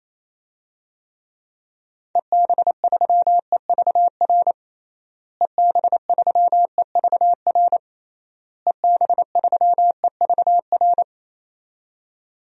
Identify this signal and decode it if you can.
Morse code, heard as a keyed tone, about 28 wpm, 700 Hz. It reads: EB3EVR EB3EVR EB3EVR
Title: DXCC-100 Call Signs - 6 Characters - Repeated 3x - 28wpm